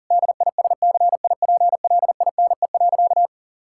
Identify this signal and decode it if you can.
Morse code, heard as a keyed tone, about 40 wpm, 700 Hz. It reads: DISCIPLINE.
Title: Sentences from Top 1000 Words - Review Part 6 - 40wpm